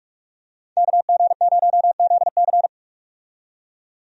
Morse code, heard as a keyed tone, about 47 wpm, 700 Hz. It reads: KG0ZC
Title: USA Call Signs - 5 Characters - 45wpm